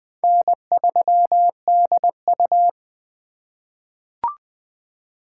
Morse code, heard as a keyed tone, about 20 wpm, 700 Hz. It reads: N3DU
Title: USA Call Signs - 4x Character Spacing - 20wpm